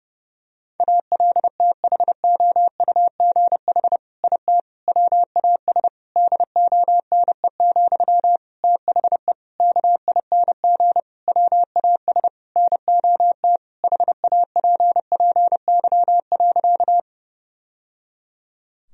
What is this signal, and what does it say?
Morse code, heard as a keyed tone, about 32 wpm, 700 Hz. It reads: ALTHOUGH IT WAS DONE, THE KING WAS NOT HAPPY.
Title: Sentences from Top 1000 Words - Review Part 3 - 30wpm